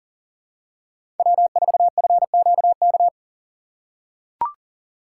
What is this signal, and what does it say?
Morse code, heard as a keyed tone, about 38 wpm, 700 Hz. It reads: W4FQK
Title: USA Call Signs - 5 Characters - 3x Character Spacing - 40wpm